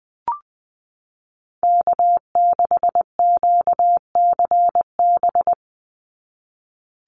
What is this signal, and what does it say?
Morse code, heard as a keyed tone, about 20 wpm, 700 Hz. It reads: K6QCB
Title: NA SOTA Chaser Callsigns 20wpm